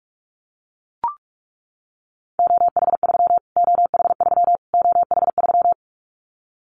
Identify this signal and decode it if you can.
Morse code, heard as a keyed tone, about 44 wpm, 700 Hz. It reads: OH3 OH3 OH3
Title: Callsign Prefixes for Top 25 DXCC - Repeated 3x - 45wpm